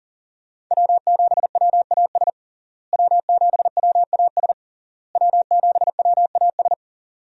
Morse code, heard as a keyed tone, about 40 wpm, 700 Hz. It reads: W7WAS W7WAS W7WAS
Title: USA Call Signs - 5 Characters - Repeated 3x - 40wpm